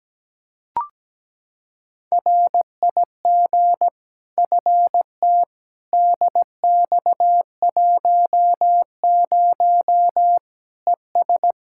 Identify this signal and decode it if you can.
Morse code, heard as a keyed tone, about 17 wpm, 700 Hz. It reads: RIG FT DX10 ES
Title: Practice QSO - Exchange 2 - Info - 17wpm